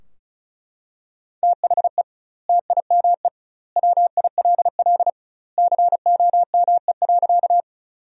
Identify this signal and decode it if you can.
Morse code, heard as a keyed tone, about 36 wpm, 700 Hz. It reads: THE TIME WILL COME.
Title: 2x Word-Spacing - Sentences from Top 100 Words - 35wpm